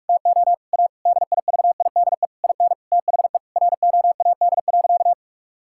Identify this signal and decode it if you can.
Morse code, heard as a keyed tone, about 46 wpm, 700 Hz. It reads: TO A DIVIDE IN THE ROAD.
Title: Sentences from Top 1600 Words 45wpm